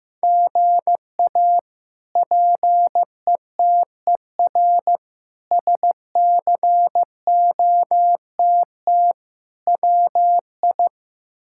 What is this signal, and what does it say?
Morse code, heard as a keyed tone, about 15 wpm, 700 Hz. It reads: GA PETER SCOTT WI
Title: K1USN SST QSO Practice - 15wpm